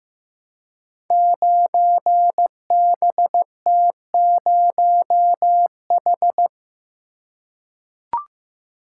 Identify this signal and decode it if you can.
Morse code, heard as a keyed tone, about 15 wpm, 700 Hz. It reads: 9BT0H
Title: Code Groups - 5 Letter:Numbers - 15wpm